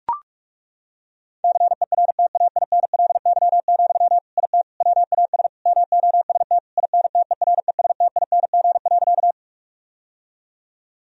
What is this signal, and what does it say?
Morse code, heard as a keyed tone, about 41 wpm, 700 Hz. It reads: CERTAINLY, IT WAS MOST INTERESTING.
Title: Sentences from Top 1000 Words - Review Part 1 - 45wpm